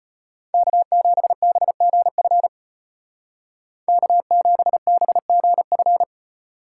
Morse code, heard as a keyed tone, about 33 wpm, 700 Hz. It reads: K7BGF K7BGF